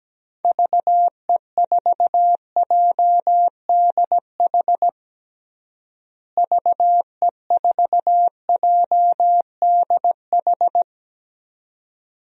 Morse code, heard as a keyed tone, about 17 wpm, 700 Hz. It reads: VE4JDH VE4JDH
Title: DXCC-100 Call Signs - 6 Characters - Repeated 2x - 17wpm